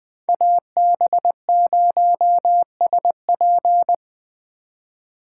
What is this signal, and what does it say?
Morse code, heard as a keyed tone, about 20 wpm, 700 Hz. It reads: AB0SP
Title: Speed Racing - USA Call Signs 17wpm